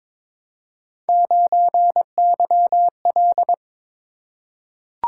Morse code, heard as a keyed tone, about 23 wpm, 700 Hz. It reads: 9YL
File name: practice / 22wpm / Three Letter-Number Combinations - 22wpm